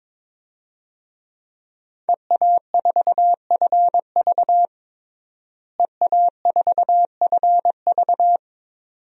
Morse code, heard as a keyed tone, about 22 wpm, 700 Hz. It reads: EA4FV EA4FV
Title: DXCC-100 Call Signs - 5 Characters - Repeated 2x - 22wpm